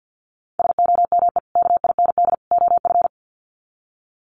Morse code, heard as a keyed tone, about 50 wpm, 700 Hz. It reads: SOME KIND OF
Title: Sets of 3 Words 50wpm